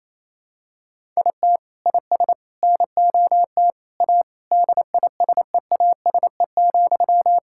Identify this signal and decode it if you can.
Morse code, heard as a keyed tone, about 28 wpm, 700 Hz. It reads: IT IS NOT A DISEASE,
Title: Sentences from Top 1000 Words - Review Part 6 - 28wpm